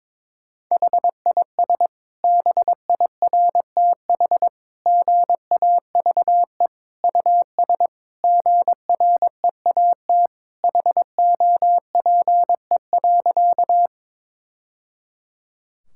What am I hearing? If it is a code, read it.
Morse code, heard as a keyed tone, about 22 wpm, 700 Hz. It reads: HIS BIRTH GAVE US GREAT HOPE.
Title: Sentences from Top 1000 Words - Review Part 5 - 22wpm